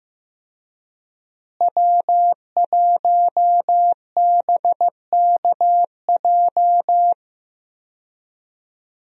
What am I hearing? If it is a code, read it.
Morse code, heard as a keyed tone, about 15 wpm, 700 Hz. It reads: W1BKJ
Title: USA Call Signs - 5 Characters - Repeated 2x - 15wpm